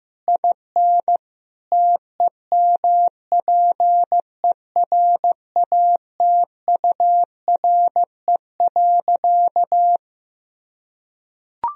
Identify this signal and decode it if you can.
Morse code, heard as a keyed tone, about 15 wpm, 700 Hz. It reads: IN TEMPERATURE.
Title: Sentences from Top 1000 Words - Review Part 4 - 15wpm